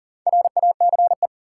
Morse code, heard as a keyed tone, about 40 wpm, 700 Hz. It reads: RACE
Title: Four Letter Words - 40wpm